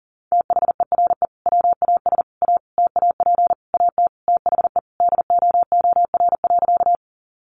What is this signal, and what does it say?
Morse code, heard as a keyed tone, about 40 wpm, 700 Hz. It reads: THERE WAS A TAP AT THE DOOR.
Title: Sentences from Top 1400 Words 40wpm